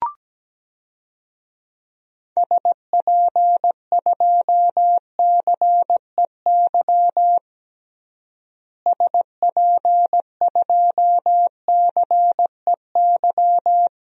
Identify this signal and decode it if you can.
Morse code, heard as a keyed tone, about 17 wpm, 700 Hz. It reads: SP2CEY SP2CEY
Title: DXCC-100 Call Signs - 6 Characters - Repeated 3x - 17wpm